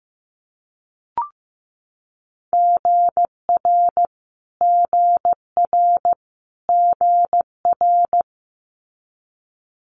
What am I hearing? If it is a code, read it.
Morse code, heard as a keyed tone, about 15 wpm, 700 Hz. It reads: GR GR GR
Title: Multi-letter Phonemes Repeated 3x 15wpm